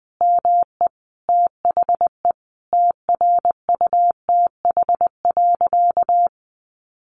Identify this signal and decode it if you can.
Morse code, heard as a keyed tone, about 20 wpm, 700 Hz. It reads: ME THE TRUTH.
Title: Sentences from Top 1500 Words 20wpm